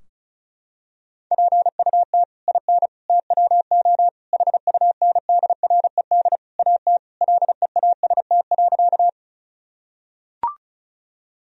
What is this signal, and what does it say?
Morse code, heard as a keyed tone, about 34 wpm, 700 Hz. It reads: PUT IN TWO HUNDRED AT LEAST.
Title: Sentences from Top 1000 Words - Review Part 2 - 35wpm